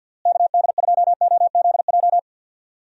Morse code, heard as a keyed tone, about 49 wpm, 700 Hz. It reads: KD2OZJ
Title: USA Call Signs - 6 Characters - Repeated 2x - 50wpm